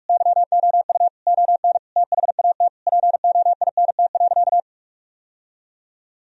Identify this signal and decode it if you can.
Morse code, heard as a keyed tone, about 47 wpm, 700 Hz. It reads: YOU ON THAT POINT.
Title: Sentences from Top 1000 Words - Review Part 1 - 45wpm